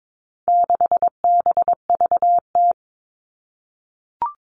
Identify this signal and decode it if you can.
Morse code, heard as a keyed tone, about 22 wpm, 700 Hz. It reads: 6BVT
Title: All Cut Numbers - 4-Digit 22wpm